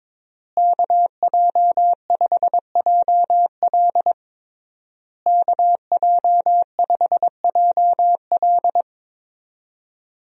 Morse code, heard as a keyed tone, about 22 wpm, 700 Hz. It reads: KJ5JL KJ5JL
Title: USA Call Signs - 5 Characters - Repeated 3x - 22wpm